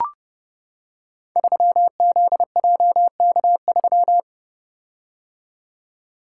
Morse code, heard as a keyed tone, about 30 wpm, 700 Hz. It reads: 3ZJK3
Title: Code Groups - 5 Letter:Numbers - 30wpm